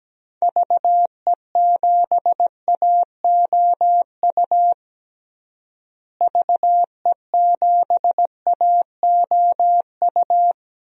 Morse code, heard as a keyed tone, about 17 wpm, 700 Hz. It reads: VE7AOU VE7AOU
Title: DXCC-100 Call Signs - 6 Characters - Repeated 2x - 17wpm